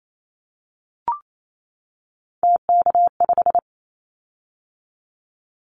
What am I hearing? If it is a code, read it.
Morse code, heard as a keyed tone, about 28 wpm, 700 Hz. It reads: TK5
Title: Callsign Prefixes for Top 100 DXCC - 28wpm